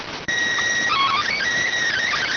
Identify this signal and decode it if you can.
SSTV (slow-scan television), PD 290.